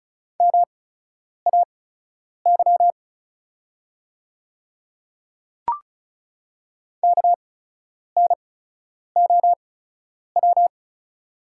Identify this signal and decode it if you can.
Morse code, heard as a keyed tone, about 36 wpm, spaced as for 10 wpm, 700 Hz. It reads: MAY KNOW
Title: 8x Character Spacing - Top 100 Words - 35wpm